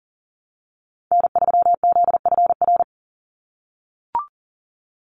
Morse code, heard as a keyed tone, about 40 wpm, 690 Hz. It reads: N3ZFR
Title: USA Call Signs 40wpm